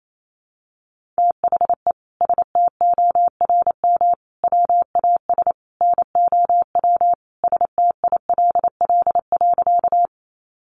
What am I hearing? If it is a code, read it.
Morse code, heard as a keyed tone, about 28 wpm, 700 Hz. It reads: THE STORM WAS NOW STILL.
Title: Sentences from Top 1500 Words 28wpm